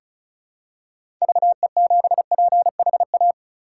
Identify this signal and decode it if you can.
Morse code, heard as a keyed tone, about 34 wpm, 700 Hz. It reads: VE7PHA
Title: K1USN SST QSO Practice - 35wpm